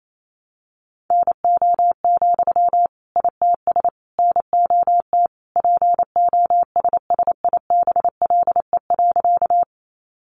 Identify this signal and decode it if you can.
Morse code, heard as a keyed tone, about 28 wpm, 700 Hz. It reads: NO, ITS NOT POSSIBLE.
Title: Sentences from Top 600 Words 28wpm